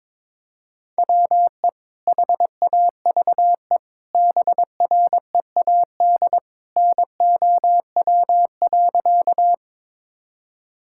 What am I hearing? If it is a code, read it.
Morse code, heard as a keyed tone, about 22 wpm, 700 Hz. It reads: WE HAVE BREAD NOW.
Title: Sentences from Top 1600 Words 22wpm